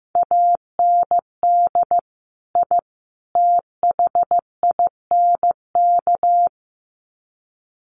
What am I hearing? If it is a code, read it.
Morse code, heard as a keyed tone, about 15 wpm, 700 Hz. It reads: AND I THINK
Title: Sets of 3 Words 15wpm